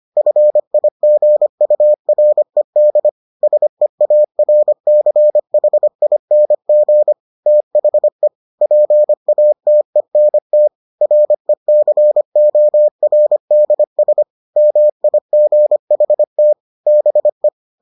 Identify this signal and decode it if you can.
Morse code, heard as a keyed tone, about 25 wpm, 600 Hz. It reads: FIGURED SEARCHING THE PATENT RECORDS MIGHT BE